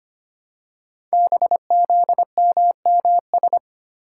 Morse code, heard as a keyed tone, about 25 wpm, 700 Hz. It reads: BZMMS
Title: Code Groups - 5 Letters - 25wpm